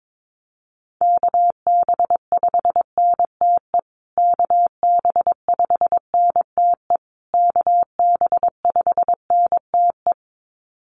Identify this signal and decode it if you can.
Morse code, heard as a keyed tone, about 22 wpm, 700 Hz. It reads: KB5NTE KB5NTE KB5NTE
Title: USA Call Signs Repeated 3x 22wpm